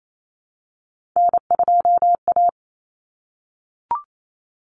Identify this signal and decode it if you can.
Morse code, heard as a keyed tone, about 28 wpm, 700 Hz. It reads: N2A
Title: Common Cut Numbers - 3-Digit 28wpm